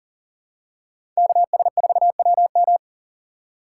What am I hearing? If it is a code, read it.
Morse code, heard as a keyed tone, about 40 wpm, 700 Hz. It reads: KS4WM
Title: USA Call Signs - 5 Characters - 6x Character Spacing - 40wpm